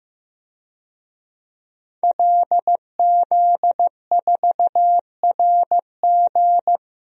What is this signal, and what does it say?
Morse code, heard as a keyed tone, about 15 wpm, 700 Hz. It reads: LZ4RG
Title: DXCC-100 Call Signs - 5 Characters - 15wpm